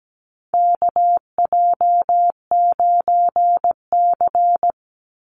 Morse code, heard as a keyed tone, about 17 wpm, 700 Hz. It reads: KJ9C
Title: Call Signs - 17wpm